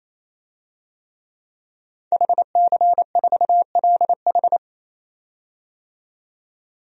Morse code, heard as a keyed tone, about 28 wpm, 700 Hz. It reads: HC4LH
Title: DXCC-100 Call Signs - 5 Characters - 28wpm